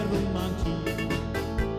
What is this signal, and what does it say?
There is no code.